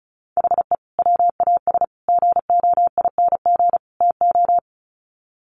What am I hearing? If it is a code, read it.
Morse code, heard as a keyed tone, about 34 wpm, 700 Hz. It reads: HE WAS GOING TO
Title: Sets of 4 Words 35wpm